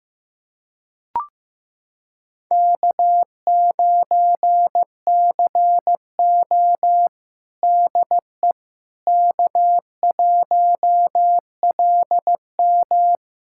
Morse code, heard as a keyed tone, about 15 wpm, 700 Hz. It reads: K9CO DE K1LM
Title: USA Callsign DE Callsign - 4 Characters - 15wpm